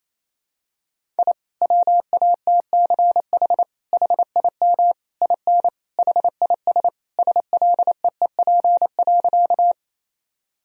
Morse code, heard as a keyed tone, about 28 wpm, 700 Hz. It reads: I WATCH HIM IN HIS SLEEP.
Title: Sentences from Top 1000 Words - Review Part 1 - 28wpm